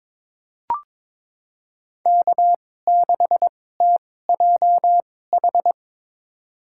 Morse code, heard as a keyed tone, about 21 wpm, spaced as for 16 wpm, 700 Hz. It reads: K6TJH
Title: USA Call Signs - 5 Characters - 2x Character Spacing - 22wpm